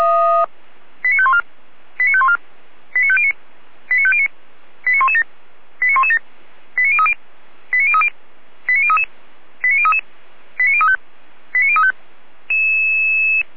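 Five-tone selective call (ZVEI), heard as quick runs of five tones, tones of about 70 ms. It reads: 89514 89514 89509 89509 89117 89117 89030 89030 89033 89033 89035 89035
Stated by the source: ZVEI five-tone call 89115 received on an SDR.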